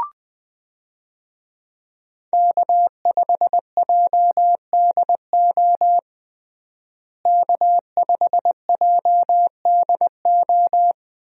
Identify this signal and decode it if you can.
Morse code, heard as a keyed tone, about 21 wpm, 700 Hz. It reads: K5JDO K5JDO
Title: USA Call Signs - 5 Characters - Repeated 3x - 20wpm